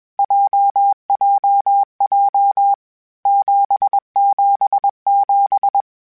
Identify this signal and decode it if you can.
Morse code, heard as a keyed tone, about 21 wpm, 800 Hz. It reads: JJJ 777
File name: Basic Sets / Basic Letter Groups Collective7 / JZ17@20